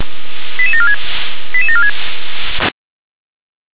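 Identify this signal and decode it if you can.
Five-tone selective call (ZVEI), heard as quick runs of five tones, tones of about 70 ms. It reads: 88746 88746
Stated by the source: ZVEI five-tone selective call received on an SDR.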